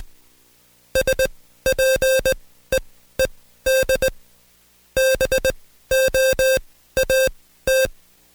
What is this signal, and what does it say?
Morse code, heard as a keyed tone, about 20 wpm, spaced as for 13 wpm, 530 Hz. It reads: SPEED BOAT
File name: Code Quick / AudioDownld2 / Seg133-Stepper6 13WPM